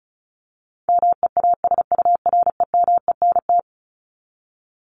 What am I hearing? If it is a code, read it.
Morse code, heard as a keyed tone, about 34 wpm, 700 Hz. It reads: MEASUREMENT
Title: Top 1751-2000 Words 35wpm